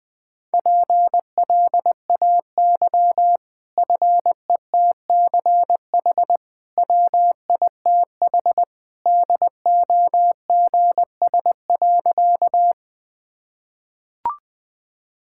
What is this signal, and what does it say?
Morse code, heard as a keyed tone, about 21 wpm, 700 Hz. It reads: PLAY FETCH WITH DOGS.
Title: Easy Sentences with Hints - 20wpm